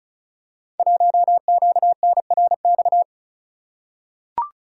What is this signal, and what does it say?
Morse code, heard as a keyed tone, about 36 wpm, 700 Hz. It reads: 1QNRX
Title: Code Groups - 5 Letter:Numbers - 35wpm